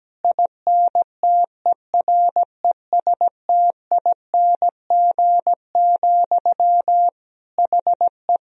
Morse code, heard as a keyed tone, about 17 wpm, 700 Hz. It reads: INTERESTING, HE
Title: Sentences from Top 1000 Words - Review Part 3 - 17wpm